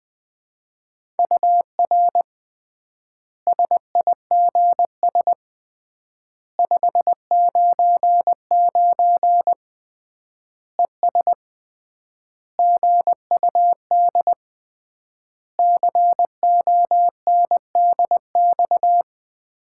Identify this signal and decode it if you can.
Morse code, heard as a keyed tone, about 21 wpm, 700 Hz. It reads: UR SIGS 599 ES GUD CONDX
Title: Common QSO Phrases - 3x Word-Spacing - 20wpm